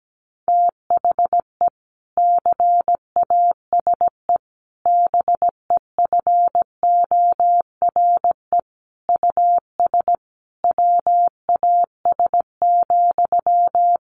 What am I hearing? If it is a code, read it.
Morse code, heard as a keyed tone, about 17 wpm, 700 Hz. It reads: THE CASE BEFORE US WAS,
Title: Sentences from Top 1400 Words 17wpm